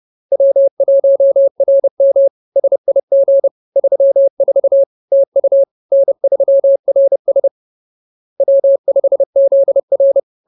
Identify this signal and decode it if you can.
Morse code, heard as a keyed tone, about 30 wpm, 550 Hz. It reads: W1RM SIG 34 TU N3RS W5ZR